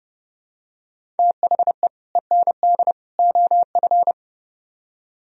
Morse code, heard as a keyed tone, about 29 wpm, 700 Hz. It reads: THE END OF